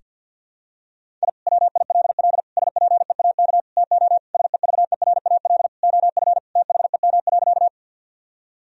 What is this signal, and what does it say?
Morse code, heard as a keyed tone, about 50 wpm, 700 Hz. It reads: I WILL SPEAK TO SEVERAL OF THEM.
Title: Sentences from Top 1000 Words - Review Part 2 - 50wpm